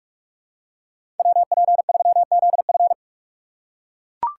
Morse code, heard as a keyed tone, about 42 wpm, 700 Hz. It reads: WP3ZF
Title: USA Call Signs - 5 Characters - 6x Character Spacing - 45wpm